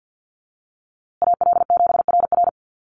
Morse code, heard as a keyed tone, about 50 wpm, 700 Hz. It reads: AL7RR
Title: USA Call Signs Repeated 3x 50wpm